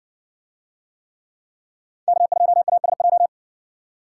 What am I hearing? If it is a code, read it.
Morse code, heard as a keyed tone, about 60 wpm, 700 Hz. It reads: K2ASJ